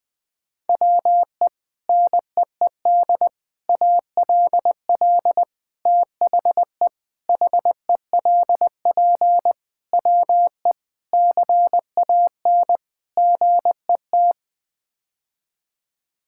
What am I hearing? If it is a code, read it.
Morse code, heard as a keyed tone, about 21 wpm, 700 Hz. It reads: WE NEED ALL THE HELP WE CAN GET
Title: Sets of 8 Words - 20wpm